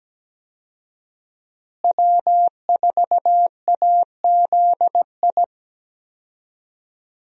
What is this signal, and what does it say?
Morse code, heard as a keyed tone, about 17 wpm, 700 Hz. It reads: W4AZI
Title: USA Call Signs - 5 Characters - Repeated 3x - 17wpm